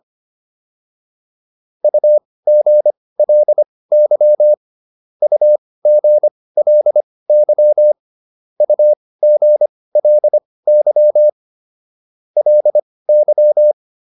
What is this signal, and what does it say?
Morse code, heard as a keyed tone, about 25 wpm, spaced as for 18 wpm, 600 Hz. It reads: UGLY UGLY UGLY LY